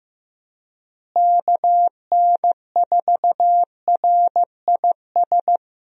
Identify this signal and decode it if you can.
Morse code, heard as a keyed tone, about 15 wpm, 700 Hz. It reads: KN4RIS
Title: USA Call Signs - 6 Characters - Repeated 3x - 15wpm